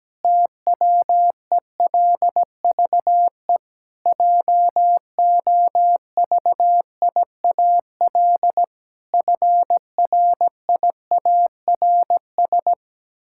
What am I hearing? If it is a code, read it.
Morse code, heard as a keyed tone, about 17 wpm, 700 Hz. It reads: TWELVE JOVIAL FRIARS
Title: Pangram Challenge v2 - 17wpm